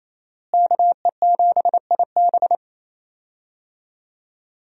Morse code, heard as a keyed tone, about 28 wpm, 700 Hz. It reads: KE7IB
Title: USA Call Signs - 5 Characters - Repeated 3x - 28wpm